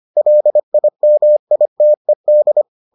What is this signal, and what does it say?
Morse code, heard as a keyed tone, about 25 wpm, 600 Hz. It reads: LIMITED